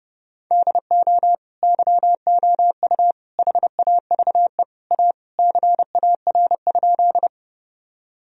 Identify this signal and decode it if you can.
Morse code, heard as a keyed tone, about 32 wpm, 700 Hz. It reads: DO YOU HAVE A CAR?
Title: Sentences from Top 1000 Words - Review Part 1 - 30wpm